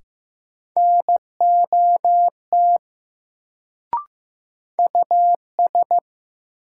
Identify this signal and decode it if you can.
Morse code, heard as a keyed tone, about 15 wpm, 700 Hz. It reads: NOT US
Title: Sets of 2 Words - New Word Drill - 15wpm